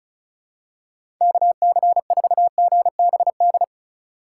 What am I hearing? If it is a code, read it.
Morse code, heard as a keyed tone, about 36 wpm, 700 Hz. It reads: KC4GBD
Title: USA Call Signs - 6 Characters - 3x Character Spacing - 35wpm